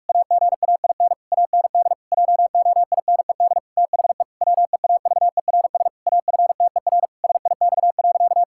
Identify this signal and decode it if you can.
Morse code, heard as a keyed tone, about 41 wpm, 700 Hz. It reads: AGAIN AND JOINED THE WEAVERS AFTER SIX.
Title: Pangram Challenge v2 - 45wpm